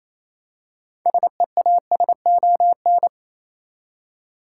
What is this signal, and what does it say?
Morse code, heard as a keyed tone, about 27 wpm, 700 Hz. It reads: SEASON
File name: Easy Words - Speed-Racing with Hints - 22wpm